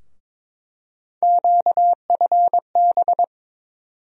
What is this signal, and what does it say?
Morse code, heard as a keyed tone, about 22 wpm, 700 Hz. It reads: QFB
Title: USA Call Sign Postfixes - 3 Letters 22wpm